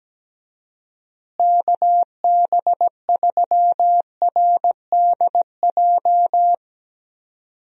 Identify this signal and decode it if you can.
Morse code, heard as a keyed tone, about 17 wpm, 700 Hz. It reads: KB3RDJ
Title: USA Call Signs - 6 Characters - Repeated 2x - 17wpm